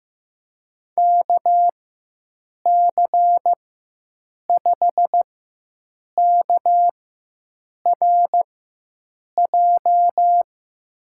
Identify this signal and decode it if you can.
Morse code, heard as a keyed tone, about 15 wpm, spaced as for 7 wpm, 700 Hz. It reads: KC5KRJ
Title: USA Call Signs - 6 Characters - 4x Character Spacing - 15wpm